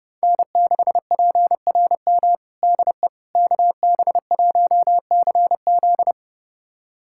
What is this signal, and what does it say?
Morse code, heard as a keyed tone, about 30 wpm, 700 Hz. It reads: N6PRM DE KB1CZ
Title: USA Callsign DE Callsign - 5 Characters - 30wpm